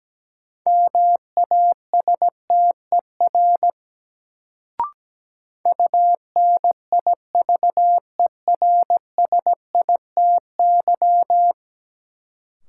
Morse code, heard as a keyed tone, about 17 wpm, 700 Hz. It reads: MASTER UNIVERSITY
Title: Sentences from Top 1300 Words - New Word Drill Alt - 17wpm